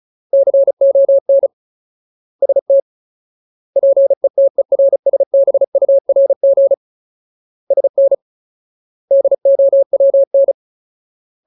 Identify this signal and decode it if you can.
Morse code, heard as a keyed tone, about 34 wpm, 550 Hz. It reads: CON ST PETERSBURG SN DOWN
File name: Ultimate Word List / 35WPM Ultimate